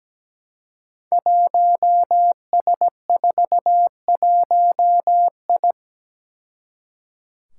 Morse code, heard as a keyed tone, about 17 wpm, 700 Hz. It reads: 1S41I
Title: Five Challenging Letter-Number Combinations 17wpm